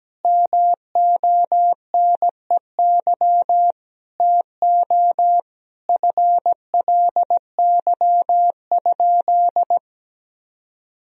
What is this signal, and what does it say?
Morse code, heard as a keyed tone, about 17 wpm, 700 Hz. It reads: MONEY TO FLY?
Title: Sentences from Top 1000 Words - Review Part 2 - 17wpm